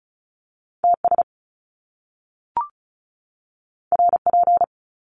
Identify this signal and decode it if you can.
Morse code, heard as a keyed tone, about 36 wpm, 700 Hz. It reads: TS RP
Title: Top Two Letter Combinations in Words 35wpm